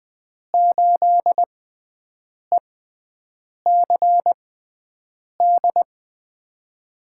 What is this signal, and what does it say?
Morse code, heard as a keyed tone, about 21 wpm, spaced as for 7 wpm, 700 Hz. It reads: 8ECD
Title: USA Call Signs - 6 Characters - 6x Character Spacing - 20wpm